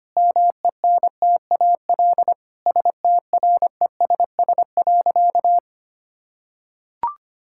Morse code, heard as a keyed tone, about 25 wpm, 700 Hz. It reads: MENTAL STRESS.
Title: Sentences from Top 1000 Words - Review Part 6 - 25wpm